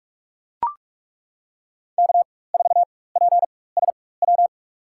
Morse code, heard as a keyed tone, about 42 wpm, spaced as for 20 wpm, 700 Hz. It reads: K4PSW
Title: USA Call Signs - 5 Characters - 4x Character Spacing - 45wpm